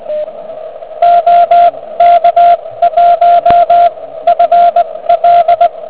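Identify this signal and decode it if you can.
Morse code, heard as a keyed tone, about 19 wpm, spaced as for 15 wpm, 690 Hz, with radio noise. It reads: OK1FL
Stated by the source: Quick exchange between two contest stations on 3559 kHz, just after a QRP contest.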